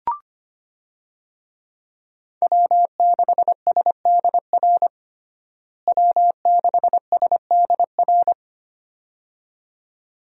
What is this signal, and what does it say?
Morse code, heard as a keyed tone, about 25 wpm, 700 Hz. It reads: W6SDR W6SDR
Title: USA Call Signs - 5 Characters - Repeated 2x - 25wpm